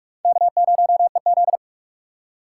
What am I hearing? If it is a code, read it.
Morse code, heard as a keyed tone, about 40 wpm, 700 Hz. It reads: K0EZ